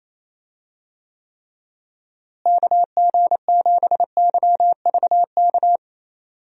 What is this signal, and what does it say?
Morse code, heard as a keyed tone, about 28 wpm, 700 Hz. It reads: KG7YVK
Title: USA Call Signs - 6 Characters - Repeated 2x - 28wpm